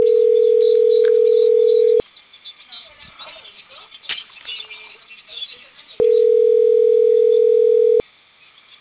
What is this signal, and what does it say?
Ringback tone (North America), 440 + 480 Hz.